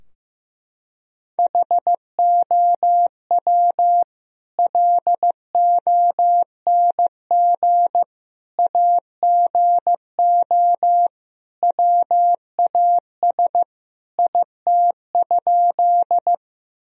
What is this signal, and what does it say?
Morse code, heard as a keyed tone, about 15 wpm, 700 Hz. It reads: HOW LONG AGO WAS IT?
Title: Sentences from Top 1000 Words - Review Part 3 - 15wpm